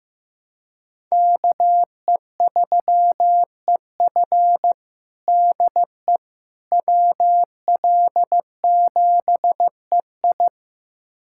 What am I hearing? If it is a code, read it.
Morse code, heard as a keyed tone, about 16 wpm, 700 Hz. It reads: KE3EF DE WL7EI
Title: USA Callsign DE Callsign - 5 Characters - 15wpm